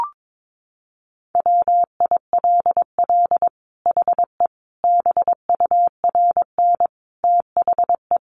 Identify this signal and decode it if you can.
Morse code, heard as a keyed tone, about 22 wpm, 700 Hz. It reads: WILL HE BURN THE
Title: Sentences from Top 1300 Words 22wpm